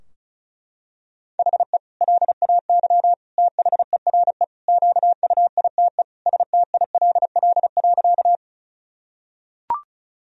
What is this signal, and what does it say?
Morse code, heard as a keyed tone, about 36 wpm, 700 Hz. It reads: HE LAY THERE QUITE STILL.
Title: Sentences from Top 1000 Words - Review Part 2 - 35wpm